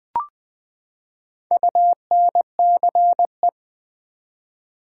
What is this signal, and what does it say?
Morse code, heard as a keyed tone, about 20 wpm, 700 Hz. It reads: UNCE
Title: Build A Word - 4 Letters At A Time - 11-Letter Words - 20wpm